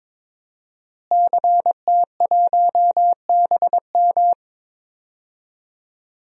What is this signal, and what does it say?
Morse code, heard as a keyed tone, about 21 wpm, 700 Hz. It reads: CT1BM
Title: DXCC-100 Call Signs - 5 Characters - Repeated 2x - 22wpm